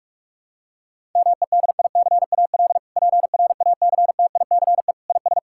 Morse code, heard as a keyed tone, about 46 wpm, 700 Hz. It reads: MEDICAL PRACTICE IS
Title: Sentences from Top 1000 Words - Review Part 5 - 45wpm